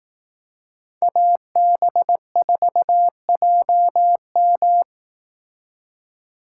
Morse code, heard as a keyed tone, about 18 wpm, 700 Hz. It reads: AB4JM
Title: Speed Racing - USA 2x2 Call Signs 15wpm